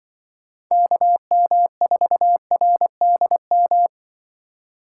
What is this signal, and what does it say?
Morse code, heard as a keyed tone, about 25 wpm, 700 Hz. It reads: KM4RDM
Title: Speed Racing - USA 2x3 Call Signs 20wpm